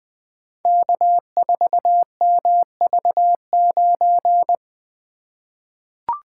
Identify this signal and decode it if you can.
Morse code, heard as a keyed tone, about 20 wpm, 700 Hz. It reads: K4MV9
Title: Code Groups - 5 Letter:Numbers - 20wpm